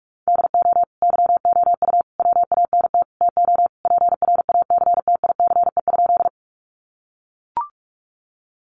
Morse code, heard as a keyed tone, about 44 wpm, 700 Hz. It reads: DO YOU WANT TO PRACTICE?
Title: Sentences from Top 700 Words 45wpm